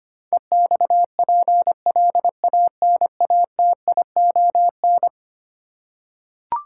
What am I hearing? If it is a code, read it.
Morse code, heard as a keyed tone, about 25 wpm, 700 Hz. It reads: EXPLANATION
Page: morse-code-ninja-practice-25wpm